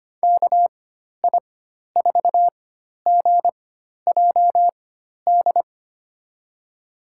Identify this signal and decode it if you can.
Morse code, heard as a keyed tone, about 25 wpm, spaced as for 12 wpm, 700 Hz. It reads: KI4GJD
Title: USA Call Signs - 6 Characters - 4x Character Spacing - 25wpm